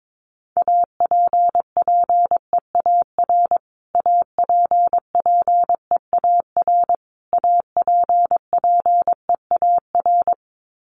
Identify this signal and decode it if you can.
Morse code, heard as a keyed tone, about 22 wpm, 700 Hz. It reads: APPEAR APPEAR APPEAR
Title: Top 500 Words Repeated 22wpm